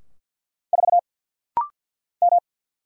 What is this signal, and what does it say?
Morse code, heard as a keyed tone, about 50 wpm, 700 Hz. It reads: 4 M
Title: Single Letter-Number - Rapid-Fire 50wpm